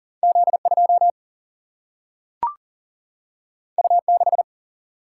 Morse code, heard as a keyed tone, about 40 wpm, 700 Hz. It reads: Z2 U6
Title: Single-Letter Single-Number 40wpm